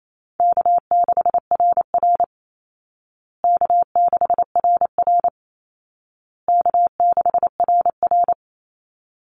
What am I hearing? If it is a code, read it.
Morse code, heard as a keyed tone, about 28 wpm, 700 Hz. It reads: K6RR K6RR K6RR
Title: Call Signs Repeated 2x - 28wpm